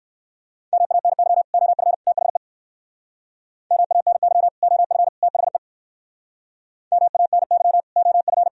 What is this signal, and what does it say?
Morse code, heard as a keyed tone, about 60 wpm, 700 Hz. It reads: MANY OF THE MANY OF THE MANY OF